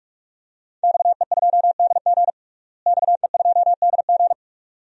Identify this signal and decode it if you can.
Morse code, heard as a keyed tone, about 43 wpm, 700 Hz. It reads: XE2DG XE2DG
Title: DXCC-100 Call Signs - 5 Characters - Repeated 2x - 45wpm